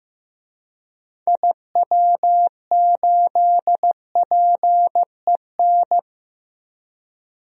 Morse code, heard as a keyed tone, about 15 wpm, 700 Hz. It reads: IW8PEN
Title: DXCC-100 Call Signs - 6 Characters - Repeated 2x - 15wpm